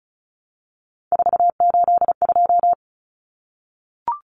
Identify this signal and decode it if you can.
Morse code, heard as a keyed tone, about 34 wpm, 700 Hz. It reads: 482
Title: Common Cut Numbers - 3-Digit 35wpm